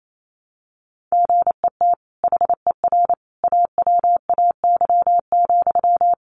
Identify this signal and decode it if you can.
Morse code, heard as a keyed tone, about 28 wpm, 700 Hz. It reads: GET HER AWAY,
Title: Sentences from Top 1500 Words 28wpm